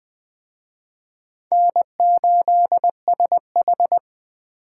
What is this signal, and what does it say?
Morse code, heard as a keyed tone, about 21 wpm, 700 Hz. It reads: N8SH
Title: Speed Racing - USA 1x2 Call Signs 17wpm